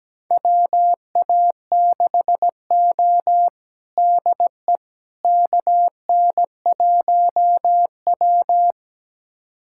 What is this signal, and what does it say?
Morse code, heard as a keyed tone, about 17 wpm, 700 Hz. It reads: WA6O DE KN1W
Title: USA Callsign DE Callsign - 4 Characters - 17wpm